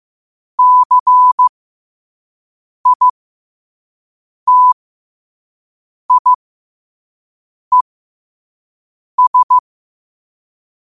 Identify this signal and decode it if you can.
Morse code, heard as a keyed tone, about 15 wpm, spaced as for 5 wpm, 1000 Hz. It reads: CITIES
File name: qotdfiles / 5wpm